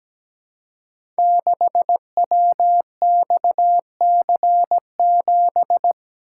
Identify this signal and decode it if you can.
Morse code, heard as a keyed tone, about 17 wpm, 700 Hz. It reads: 6WXC7
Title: Code Groups - 5 Letter:Numbers - 17wpm